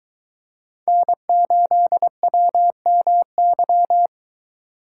Morse code, heard as a keyed tone, about 22 wpm, 700 Hz. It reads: N8WMY